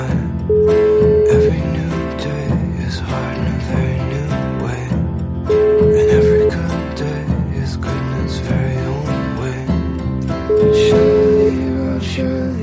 Ringback tone (Europe, 425 Hz), with noise.